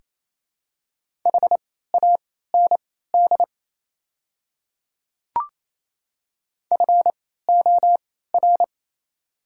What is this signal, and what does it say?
Morse code, heard as a keyed tone, about 27 wpm, spaced as for 16 wpm, 700 Hz. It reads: HAND FOR
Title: 3x Character Spacing - Top 200 Words - 28wpm